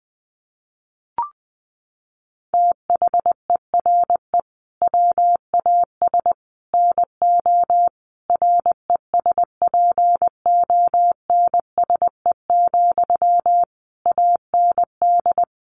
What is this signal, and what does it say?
Morse code, heard as a keyed tone, about 20 wpm, 700 Hz. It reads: THERE WAS NO RESPONSE, AND
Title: Sentences from Top 1400 Words 20wpm